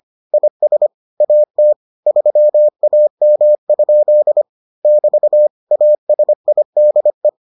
Morse code, heard as a keyed tone, about 25 wpm, 600 Hz. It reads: IS AT 3AM? <BT> ASIDE